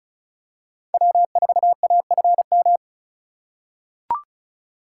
Morse code, heard as a keyed tone, about 36 wpm, 700 Hz. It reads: W4AFM
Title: USA Call Signs - 5 Characters - 2x Character Spacing - 35wpm